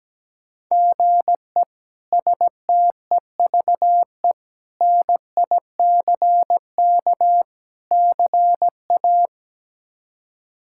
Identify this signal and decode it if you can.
Morse code, heard as a keyed tone, about 17 wpm, 700 Hz. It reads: GE STEVE NICK CA
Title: K1USN SST QSO Practice - 17wpm